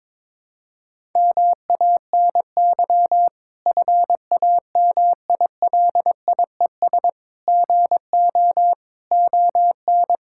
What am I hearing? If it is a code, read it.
Morse code, heard as a keyed tone, about 22 wpm, 700 Hz. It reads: MANY FAMILIES GO ON
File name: Difficult Sentences with Hints - 22wpm